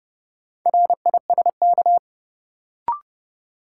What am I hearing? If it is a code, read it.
Morse code, heard as a keyed tone, about 30 wpm, 700 Hz. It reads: RISK
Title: Sentences from Top 600 Words - New Word Drill - 30wpm